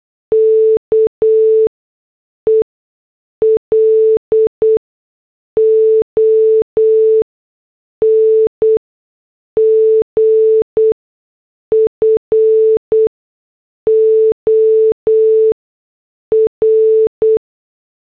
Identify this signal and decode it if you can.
Morse code, heard as a keyed tone, about 8 wpm, spaced as for 6 wpm, 440 Hz. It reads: KE LONG FOR